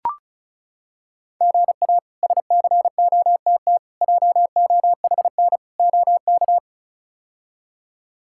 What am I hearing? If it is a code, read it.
Morse code, heard as a keyed tone, about 36 wpm, 700 Hz. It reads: GA SCOTT JOHN OK
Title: K1USN SST QSO Practice - 35wpm